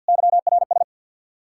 Morse code, heard as a keyed tone, about 64 wpm, 700 Hz. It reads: YRS